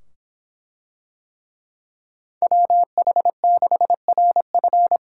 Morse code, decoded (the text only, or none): WH6RF